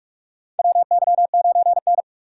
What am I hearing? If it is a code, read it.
Morse code, heard as a keyed tone, about 41 wpm, 700 Hz. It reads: WY9N
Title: USA Call Signs - 4 Characters - 2x Character Spacing - 45wpm